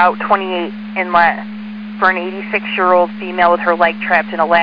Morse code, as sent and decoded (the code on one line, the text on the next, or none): none